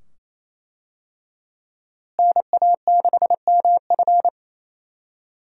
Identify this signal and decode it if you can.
Morse code, heard as a keyed tone, about 27 wpm, 700 Hz. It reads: NA6MF
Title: USA Call Signs - 5 Characters - Repeated 2x - 28wpm